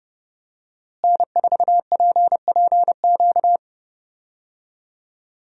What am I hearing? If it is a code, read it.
Morse code, heard as a keyed tone, about 32 wpm, 700 Hz. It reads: N4PPQ
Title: USA Call Signs - 5 Characters - Repeated 3x - 30wpm